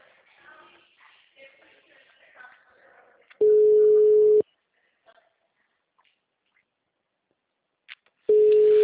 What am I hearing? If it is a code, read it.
Ringback tone (Europe, 425 Hz).